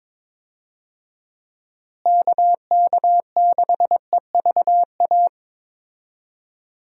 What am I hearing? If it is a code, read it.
Morse code, heard as a keyed tone, about 23 wpm, 700 Hz. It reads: KK6EVA
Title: USA Call Signs - 6 Characters - Repeated 2x - 22wpm